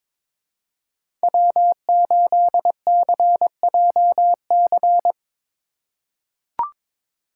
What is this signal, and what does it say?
Morse code, heard as a keyed tone, about 22 wpm, 700 Hz. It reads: W8CJC
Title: USA Call Signs - 5 Characters - 4x Character Spacing - 22wpm